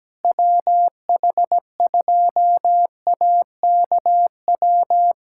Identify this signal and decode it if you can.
Morse code, heard as a keyed tone, about 17 wpm, 700 Hz. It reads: WH2AKW
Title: USA Callsign DE Callsign - 6 Characters - 17wpm